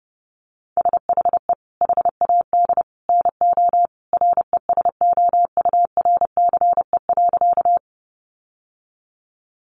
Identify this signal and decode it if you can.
Morse code, heard as a keyed tone, about 28 wpm, 700 Hz. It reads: SHE HAD NO RESOURCE.
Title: Sentences from Top 1300 Words 30wpm